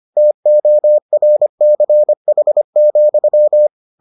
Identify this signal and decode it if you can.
Morse code, heard as a keyed tone, about 25 wpm, 600 Hz. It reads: TORCH,